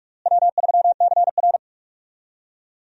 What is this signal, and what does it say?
Morse code, heard as a keyed tone, about 46 wpm, 700 Hz. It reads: W3CR
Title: USA Call Signs - 4 Characters - Repeated 2x - 45wpm